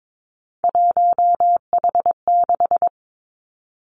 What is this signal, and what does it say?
Morse code, heard as a keyed tone, about 22 wpm, 700 Hz. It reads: 1H6
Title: Three Challenging Letter-Number Combinations 22wpm